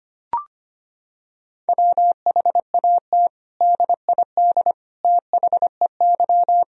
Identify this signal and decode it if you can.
Morse code, heard as a keyed tone, about 25 wpm, 700 Hz. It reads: WHAT DID THEY
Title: Sentences from Top 1000 Words - Review Part 4 - 25wpm